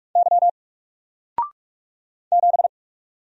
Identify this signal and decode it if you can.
Morse code, heard as a keyed tone, about 44 wpm, 700 Hz. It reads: Y 7
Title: Single Letter-Number 45wpm